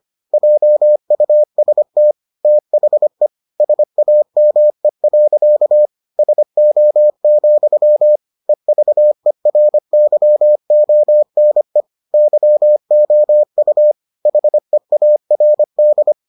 Morse code, heard as a keyed tone, about 25 wpm, 600 Hz. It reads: JUST THE SAME. SO, EVERYONE YOU HEARD